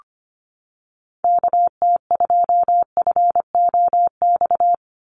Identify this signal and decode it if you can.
Morse code, heard as a keyed tone, about 25 wpm, 690 Hz. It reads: KT2FOX
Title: USA Call Signs 25wpm